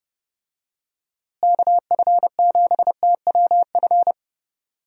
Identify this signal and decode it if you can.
Morse code, heard as a keyed tone, about 30 wpm, 700 Hz. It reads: KF7TWF
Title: USA Call Signs - 6 Characters - Repeated 3x - 30wpm